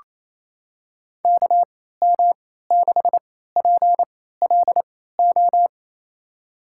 Morse code, heard as a keyed tone, about 28 wpm, spaced as for 16 wpm, 700 Hz. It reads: KM6PLO
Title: USA Call Signs - 3x Character Spacing - 28wpm